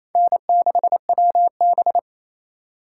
Morse code, heard as a keyed tone, about 35 wpm, 700 Hz. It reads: N6WB
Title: Speed Racing - USA 1x2 Call Signs 28wpm